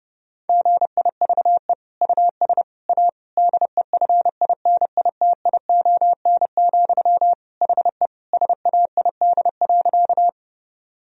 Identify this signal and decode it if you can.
Morse code, heard as a keyed tone, about 32 wpm, 700 Hz. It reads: GIVE US A DEFINITION, HE SAID.
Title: Sentences from Top 1000 Words - Review Part 4 - 30wpm